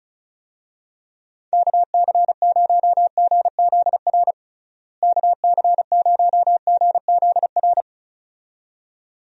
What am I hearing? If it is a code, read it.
Morse code, heard as a keyed tone, about 36 wpm, 700 Hz. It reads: KC0GZR KC0GZR
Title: USA Call Signs - 6 Characters - Repeated 2x - 35wpm